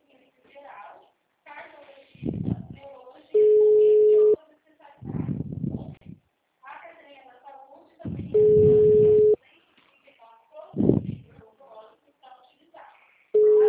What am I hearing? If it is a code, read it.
Ringback tone (Europe, 425 Hz), with speech.
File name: out- -2010- - - .85163